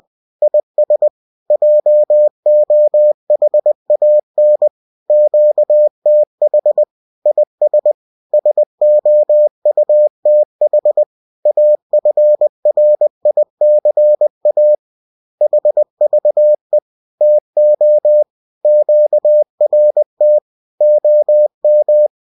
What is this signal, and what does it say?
Morse code, heard as a keyed tone, about 20 wpm, 600 Hz. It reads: IS JOHAN QTH IS SOUTH AFRICA HVE TO QRT OM